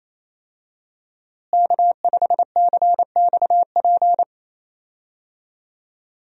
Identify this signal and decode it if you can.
Morse code, heard as a keyed tone, about 27 wpm, 700 Hz. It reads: K5CXP
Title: USA Call Signs - 5 Characters - Repeated 3x - 28wpm